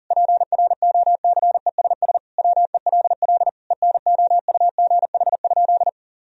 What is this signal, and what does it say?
Morse code, heard as a keyed tone, about 37 wpm, 700 Hz. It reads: PROCESS WELL ENOUGH?
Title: Sentences from Top 1000 Words - Review Part 3 - 40wpm